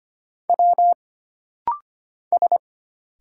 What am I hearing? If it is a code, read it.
Morse code, heard as a keyed tone, about 25 wpm, 700 Hz. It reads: W S
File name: Single Character (Letter:Number:Punctuation:Prosign) - Rapid-Fire - 25wpm